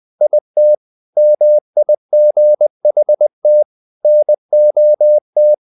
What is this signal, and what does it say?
Morse code, heard as a keyed tone, about 20 wpm, 600 Hz. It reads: IT MIGHT NOT
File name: the old man and the sea / the old man and the sea part 1 20wpm